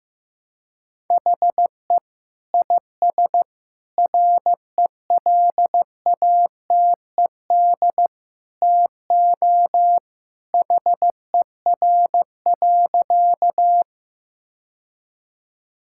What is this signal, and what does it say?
Morse code, heard as a keyed tone, about 15 wpm, 700 Hz. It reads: HE IS RELATED TO HER.
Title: Sentences from Top 1000 Words - Review Part 5 - 15wpm